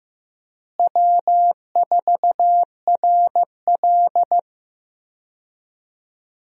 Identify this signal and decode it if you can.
Morse code, heard as a keyed tone, about 16 wpm, 700 Hz. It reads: W4RL
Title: USA Call Signs - 4 Characters - Repeated 2x - 15wpm